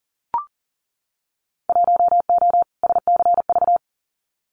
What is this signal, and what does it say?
Morse code, heard as a keyed tone, about 41 wpm, 700 Hz. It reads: 1O SCV
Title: Winter Field Day Exchanges 40wpm